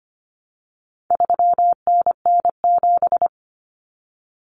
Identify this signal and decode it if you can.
Morse code, heard as a keyed tone, about 25 wpm, 700 Hz. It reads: 3NN7
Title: Common Cut Numbers - 4-Digit 25wpm